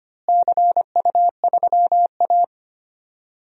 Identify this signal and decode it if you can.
Morse code, heard as a keyed tone, about 25 wpm, 700 Hz. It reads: CU3A
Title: DXCC-100 Call Signs - 4 Characters - Repeated 2x - 25wpm